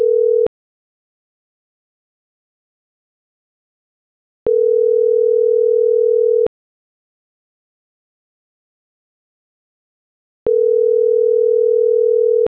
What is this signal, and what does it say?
Ringback tone (North America), 440 + 480 Hz.